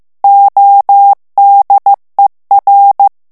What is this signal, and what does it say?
Morse code, heard as a keyed tone, about 15 wpm, 800 Hz. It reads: ODER
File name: morse mp3 / ALLEMAND texte clair 1 vitesse 14 mots minute